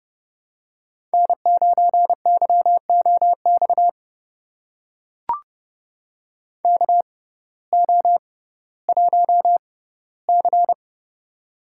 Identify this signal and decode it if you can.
Morse code, heard as a keyed tone, about 30 wpm, spaced as for 17 wpm, 700 Hz. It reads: N9YOX KO1C
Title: USA Call Signs - 6x Character Spacing - 30wpm